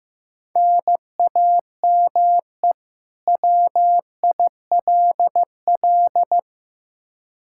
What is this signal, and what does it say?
Morse code, heard as a keyed tone, about 16 wpm, 700 Hz. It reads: NAME WILL
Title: K1USN SST QSO Practice 15wpm